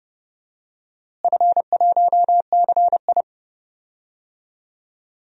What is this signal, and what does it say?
Morse code, heard as a keyed tone, about 31 wpm, 700 Hz. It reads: F1CI
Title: DXCC-100 Call Signs - 4 Characters - Repeated 3x - 30wpm